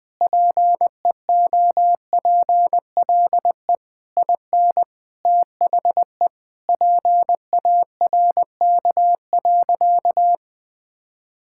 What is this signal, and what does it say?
Morse code, heard as a keyed tone, about 20 wpm, 700 Hz. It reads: PEOPLE IN THE PARK.
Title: Sentences with 3-Word Phrases - Spoken Beginning - 20wpm